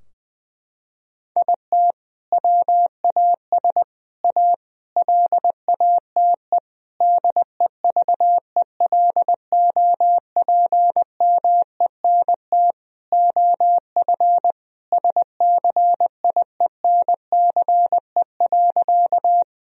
Morse code, heard as a keyed tone, about 20 wpm, 700 Hz. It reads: IT WAS A LATE DEVELOPMENT OF SCIENCE.
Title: Sentences from Top 1000 Words - Review Part 4 - 20wpm